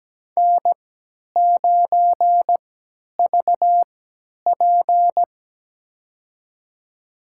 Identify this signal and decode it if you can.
Morse code, heard as a keyed tone, about 17 wpm, spaced as for 10 wpm, 700 Hz. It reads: N9VP
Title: USA Call Signs - 3x Character Spacing - 17wpm